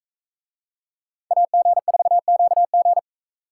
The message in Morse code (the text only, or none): AG4QG